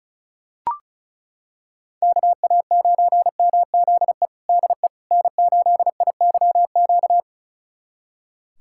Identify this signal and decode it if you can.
Morse code, heard as a keyed tone, about 36 wpm, 700 Hz. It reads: KA9MZE DE N8IYQ
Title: USA Callsign DE Callsign 35wpm